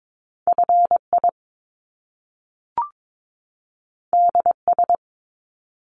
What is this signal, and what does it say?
Morse code, heard as a keyed tone, about 22 wpm, 700 Hz. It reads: FI DS
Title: Top Two Letter Combinations in Words 22wpm